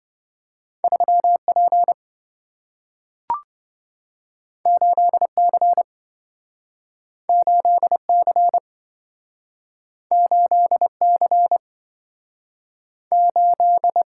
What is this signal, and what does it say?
Morse code, heard as a keyed tone, about 27 wpm, 700 Hz. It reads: 3P 8C 8C 8C 8